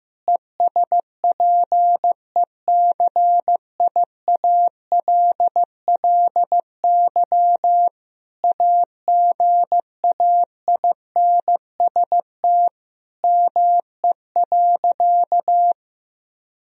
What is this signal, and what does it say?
Morse code, heard as a keyed tone, about 15 wpm, 700 Hz. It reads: ESPECIALLY AGAINST ME.
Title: Sentences from Top 1000 Words - Review Part 4 - 15wpm